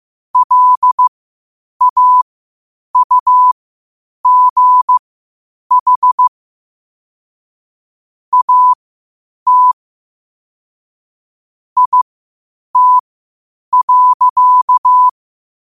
Morse code, heard as a keyed tone, about 15 wpm, spaced as for 8 wpm, 1000 Hz. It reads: LAUGH AT IT.
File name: qotdfiles / 8wpm